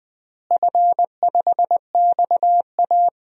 Morse code, heard as a keyed tone, about 20 wpm, 700 Hz. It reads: F5XA
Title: DXCC-100 Call Signs - 4 Characters - Repeated 3x - 20wpm